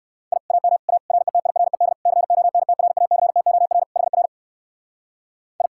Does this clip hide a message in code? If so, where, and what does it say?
Yes: Morse code from 0.3 to 5.7 s, heard as a keyed tone, about 57 wpm, 700 Hz. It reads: I AM A RETIRED CONTRACTOR HR I